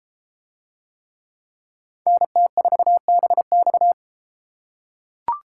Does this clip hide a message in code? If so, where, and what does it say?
Yes: Morse code from 2.1 to 3.9 s, heard as a keyed tone, about 32 wpm, 700 Hz. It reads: NT4BX